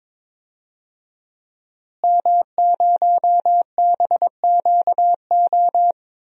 Morse code, heard as a keyed tone, about 22 wpm, 700 Hz. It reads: M0BQO
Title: DXCC-100 Call Signs - 5 Characters - Repeated 2x - 22wpm